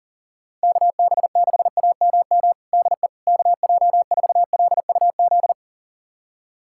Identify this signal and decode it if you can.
Morse code, heard as a keyed tone, about 40 wpm, 700 Hz. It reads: KB6AMM DE KJ4LUZ
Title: USA Callsign DE Callsign - 6 Characters - 40wpm